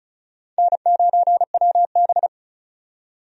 Morse code, heard as a keyed tone, about 36 wpm, 700 Hz. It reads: N9WB
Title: USA Call Signs - 4 Characters - 35wpm